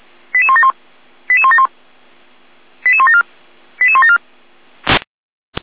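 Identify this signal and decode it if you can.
Five-tone selective call (ZVEI), heard as quick runs of five tones, tones of about 70 ms. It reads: 88171 88171 88174 88174
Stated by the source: ZVEI five-tone selective call received by SDR.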